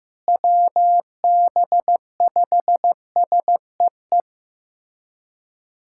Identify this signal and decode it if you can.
Morse code, heard as a keyed tone, about 15 wpm, 700 Hz. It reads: WB5SEE
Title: USA Callsign DE Callsign - 6 Characters - 15wpm